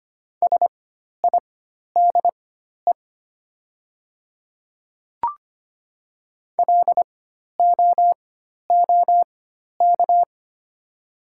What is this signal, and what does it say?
Morse code, heard as a keyed tone, about 25 wpm, spaced as for 12 wpm, 700 Hz. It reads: SIDE LOOK
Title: 4x Character Spacing - Top 100 Words - 25wpm